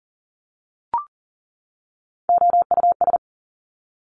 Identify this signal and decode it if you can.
Morse code, heard as a keyed tone, about 38 wpm, 700 Hz. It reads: OUS
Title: Top Three Letter Combinations in Words 40wpm